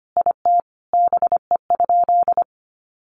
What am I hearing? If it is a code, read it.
Morse code, heard as a keyed tone, about 25 wpm, 700 Hz. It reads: IT BE?
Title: Sentences from Top 1200 Words 25wpm